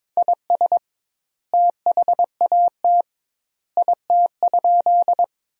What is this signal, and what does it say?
Morse code, heard as a keyed tone, about 22 wpm, 700 Hz. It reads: IS THAT IT?
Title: 2x Word-Spacing - Sentences from Top 100 Words - 22wpm